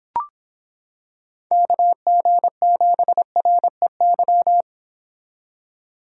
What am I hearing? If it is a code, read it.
Morse code, heard as a keyed tone, about 25 wpm, 700 Hz. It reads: KG7REY